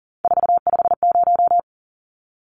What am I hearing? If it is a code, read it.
Morse code, heard as a keyed tone, about 40 wpm, 700 Hz. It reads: 450
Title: Numbers - 3-Digit 40wpm